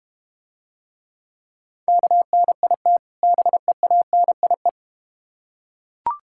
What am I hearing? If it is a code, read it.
Morse code, heard as a keyed tone, about 32 wpm, 700 Hz. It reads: KNIT BEANIE